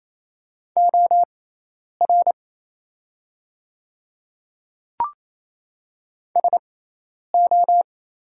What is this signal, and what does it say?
Morse code, heard as a keyed tone, about 27 wpm, spaced as for 10 wpm, 690 Hz. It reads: OR SO
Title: 6x Character Spacing - Top 100 Words - 28wpm